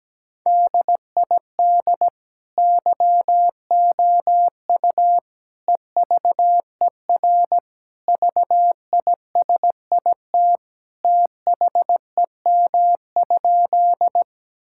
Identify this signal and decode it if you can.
Morse code, heard as a keyed tone, about 17 wpm, 700 Hz. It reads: DID YOU EVER VISIT THEM?
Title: Sentences from Top 1000 Words - Review Part 6 - 17wpm